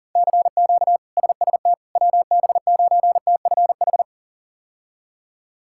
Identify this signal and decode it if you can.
Morse code, heard as a keyed tone, about 37 wpm, 700 Hz. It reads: CQ SST WB9TFH
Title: K1USN SST QSO Practice - 40wpm